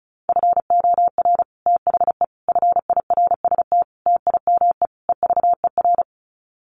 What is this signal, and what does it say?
Morse code, heard as a keyed tone, about 34 wpm, 700 Hz. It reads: FOR THE FIRST TIME EVER
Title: Sets of 5 Words Encore2 35wpm